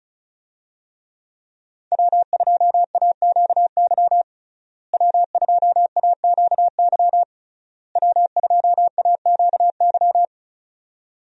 Morse code, heard as a keyed tone, about 36 wpm, 700 Hz. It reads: W2AQY W2AQY W2AQY
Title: USA Call Signs - 5 Characters - Repeated 3x - 35wpm